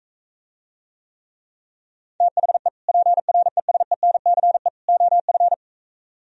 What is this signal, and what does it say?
Morse code, heard as a keyed tone, about 45 wpm, 700 Hz. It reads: THE PRESENCE OF